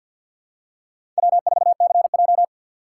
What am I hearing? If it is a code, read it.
Morse code, heard as a keyed tone, about 52 wpm, 700 Hz. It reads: W4CJ
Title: USA Call Signs - 4 Characters - 3x Character Spacing - 50wpm